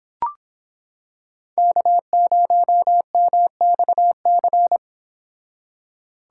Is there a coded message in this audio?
Yes: Morse code from 1.6 to 4.8 s, heard as a keyed tone, about 26 wpm, 700 Hz. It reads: K0MXC